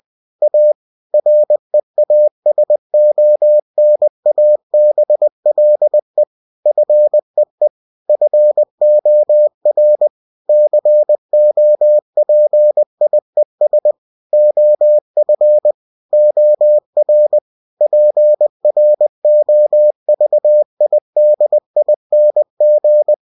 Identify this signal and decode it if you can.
Morse code, heard as a keyed tone, about 20 wpm, 600 Hz. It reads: A REASONABLE FEE FOR COPIES OF OR PROVIDING
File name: cw-books / winnie-the-pooh / winnie-20wpm-page-0116